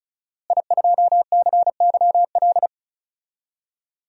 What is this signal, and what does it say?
Morse code, heard as a keyed tone, about 36 wpm, 700 Hz. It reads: I2CYL
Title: DXCC-100 Call Signs - 5 Characters - Repeated 3x - 35wpm